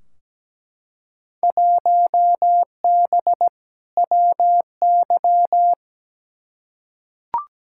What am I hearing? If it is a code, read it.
Morse code, heard as a keyed tone, about 17 wpm, 700 Hz. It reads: 1B WY
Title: Field Day Exchanges Encore - 17wpm